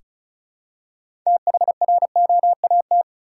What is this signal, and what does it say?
Morse code, heard as a keyed tone, about 36 wpm, 700 Hz. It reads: THROAT
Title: Six Letter Words - 35wpm